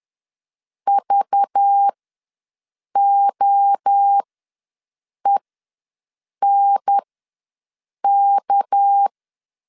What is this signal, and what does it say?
Morse code, heard as a keyed tone, about 10 wpm, spaced as for 6 wpm, 800 Hz. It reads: VOENK
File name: LEZIONI / 6 PKX0 / 6 PK ws 10wpm 3745